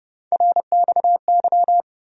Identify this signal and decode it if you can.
Morse code, heard as a keyed tone, about 30 wpm, 700 Hz. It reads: RXY
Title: USA Call Sign Postfixes - 3 Letters 30wpm